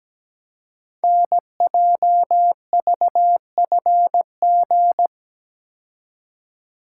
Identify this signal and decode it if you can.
Morse code, heard as a keyed tone, about 17 wpm, 700 Hz. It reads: NJVFG
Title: Code Groups - 5 Letters - 17wpm